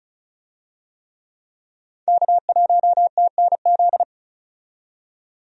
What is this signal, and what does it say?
Morse code, heard as a keyed tone, about 36 wpm, 700 Hz. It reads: K1TNZ
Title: USA Call Signs - 5 Characters - 35wpm